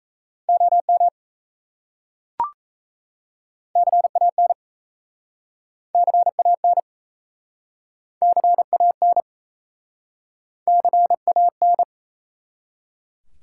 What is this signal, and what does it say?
Morse code, heard as a keyed tone, about 38 wpm, 700 Hz. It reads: OM CAN CAN CAN CAN